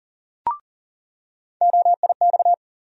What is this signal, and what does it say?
Morse code, heard as a keyed tone, about 40 wpm, 700 Hz. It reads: OIX
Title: USA Call Sign Postfixes - 3 Letters 40wpm